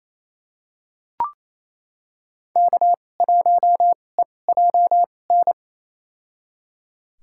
Morse code, heard as a keyed tone, about 27 wpm, spaced as for 20 wpm, 700 Hz. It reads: K1EJN
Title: USA Call Signs - 5 Characters - 2x Character Spacing - 28wpm